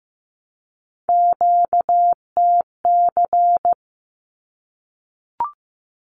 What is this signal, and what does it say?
Morse code, heard as a keyed tone, about 15 wpm, 700 Hz. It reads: QTC
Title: Q-Codes 15wpm